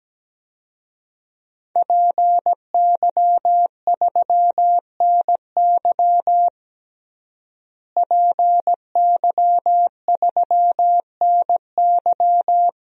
Morse code, heard as a keyed tone, about 17 wpm, 700 Hz. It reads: PY3NY PY3NY
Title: DXCC-100 Call Signs - 5 Characters - Repeated 2x - 17wpm